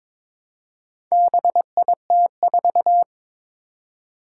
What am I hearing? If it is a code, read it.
Morse code, heard as a keyed tone, about 22 wpm, 700 Hz. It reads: BIT4
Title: Four Challenging Letter-Number Combinations 22wpm